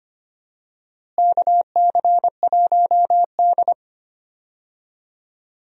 Morse code, heard as a keyed tone, about 25 wpm, 700 Hz. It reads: KC1D
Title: USA Call Signs - 4 Characters - Repeated 2x - 25wpm